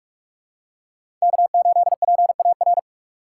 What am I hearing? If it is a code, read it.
Morse code, heard as a keyed tone, about 42 wpm, 700 Hz. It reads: K8PAR
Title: USA Call Signs - 5 Characters - 4x Character Spacing - 45wpm